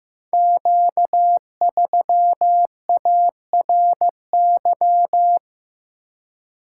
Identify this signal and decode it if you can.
Morse code, heard as a keyed tone, about 15 wpm, 700 Hz. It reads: Q3ARY
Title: Code Groups - 5 Letter:Numbers - 15wpm